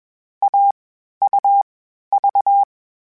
Morse code, heard as a keyed tone, about 21 wpm, 800 Hz. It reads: A U V
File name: Difficult Combos / Difficult Combos-mp3s / AUV AUV4 @20